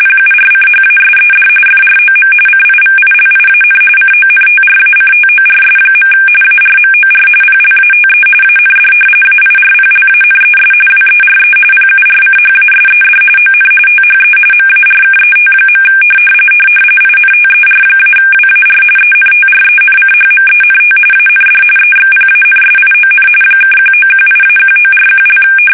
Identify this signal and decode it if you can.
RTTY, about 45.45 baud. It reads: PURCHASED FROM CIVILIAN CONTRACTORS. ARMOR PLATE AND AIRPLANES WERE PURCHASED ON THE MARKET. INTER-WAR ENTRENCHMENT AND EXPANSION (1918-1941)
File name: fsk850